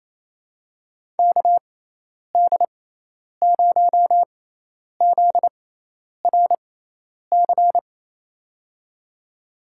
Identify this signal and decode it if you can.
Morse code, heard as a keyed tone, about 28 wpm, spaced as for 10 wpm, 700 Hz. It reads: KD0ZRC
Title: USA Call Signs - 6 Characters - 6x Character Spacing - 28wpm